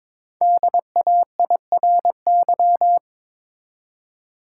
Morse code, heard as a keyed tone, about 22 wpm, 700 Hz. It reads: DAIRY
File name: 5 Letter Words - First 3 Letters Given - 22wpm